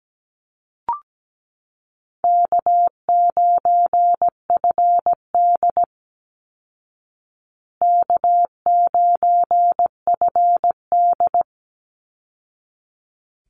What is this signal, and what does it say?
Morse code, heard as a keyed tone, about 17 wpm, 700 Hz. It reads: K9FD K9FD
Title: Call Signs Repeated 3x - 17wpm